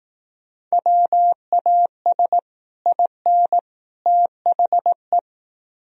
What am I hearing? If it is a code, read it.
Morse code, heard as a keyed tone, about 18 wpm, 700 Hz. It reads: WAS IN THE
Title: Speed-Racing - Sets of 3 Words 15wpm